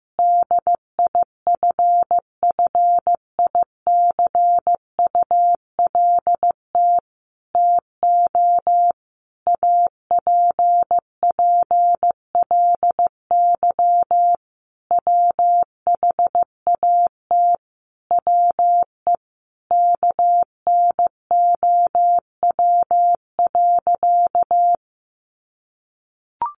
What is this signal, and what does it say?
Morse code, heard as a keyed tone, about 15 wpm, 700 Hz. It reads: DIFFICULT TO APPLY WHAT WE KNOW.
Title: Sentences from Top 1200 Words 15wpm